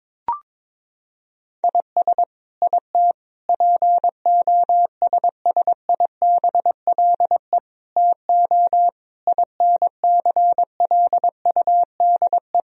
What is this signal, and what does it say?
Morse code, heard as a keyed tone, about 21 wpm, 700 Hz. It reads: IS IT POSSIBLE TO INCLUDE
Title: Sentences from Top 1000 Words - Review Part 3 - 22wpm